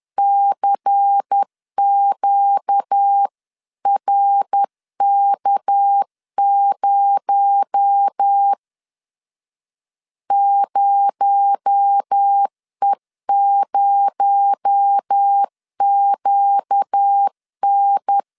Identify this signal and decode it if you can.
Morse code, heard as a keyed tone, about 10 wpm, 800 Hz. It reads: CQRK0 0E0QN
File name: LEZIONI / 6 PKX0 / 6 PKX0 10wpm 2707